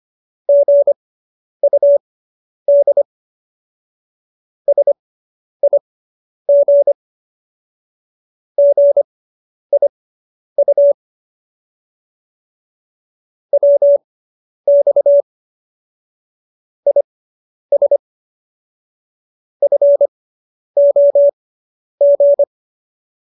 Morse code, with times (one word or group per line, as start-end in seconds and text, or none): 0.5-3.0 GUD
4.7-6.9 SIG
8.6-10.9 GIU
13.5-15.2 WX
16.9-18.0 IS
19.6-22.4 FOG